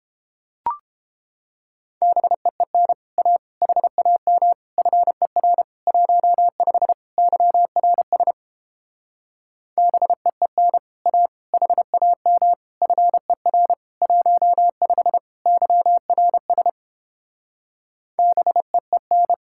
Morse code, heard as a keyed tone, about 29 wpm, 700 Hz. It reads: BEEN A HAM FER 15 YRS BEEN A HAM FER 15 YRS BEEN